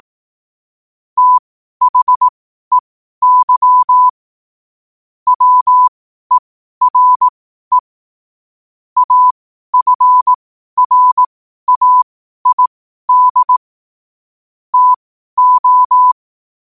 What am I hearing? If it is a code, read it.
Morse code, heard as a keyed tone, about 18 wpm, spaced as for 12 wpm, 1000 Hz. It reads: THEY WERE AFRAID TO
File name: qotdfiles / 12wpm